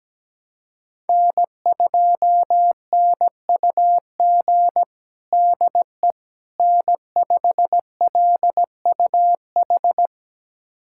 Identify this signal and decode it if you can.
Morse code, heard as a keyed tone, about 17 wpm, 700 Hz. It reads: N2NUG DE N5LUH
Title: USA Callsign DE Callsign - 5 Characters - 17wpm